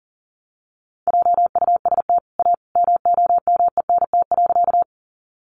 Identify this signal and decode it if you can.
Morse code, heard as a keyed tone, about 40 wpm, 700 Hz. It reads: JUST A MOMENT.
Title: Sentences from Top 1000 Words 40wpm